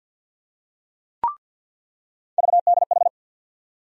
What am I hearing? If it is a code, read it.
Morse code, heard as a keyed tone, about 48 wpm, 700 Hz. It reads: VDH
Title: USA Call Sign Postfixes - 3 Letters 50wpm